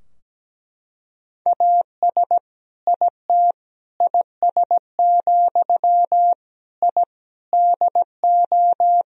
Morse code, heard as a keyed tone, about 17 wpm, 700 Hz. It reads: AS IT IS, I DO
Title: Sentences from Top 1000 Words - Review Part 2 - 17wpm